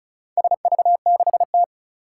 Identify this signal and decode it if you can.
Morse code, heard as a keyed tone, about 36 wpm, 690 Hz. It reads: SV6T
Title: Four Challenging Letter-Number Combinations 35wpm